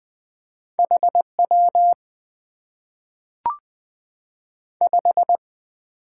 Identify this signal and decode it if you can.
Morse code, heard as a keyed tone, about 20 wpm, 700 Hz. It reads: HW 5
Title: Top 90 QSO Elements - 8x Character Spacing - 20wpm